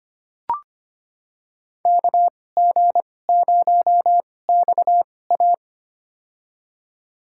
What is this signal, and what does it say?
Morse code, heard as a keyed tone, about 25 wpm, spaced as for 18 wpm, 700 Hz. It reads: KG0XA
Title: USA Call Signs - 5 Characters - 2x Character Spacing - 25wpm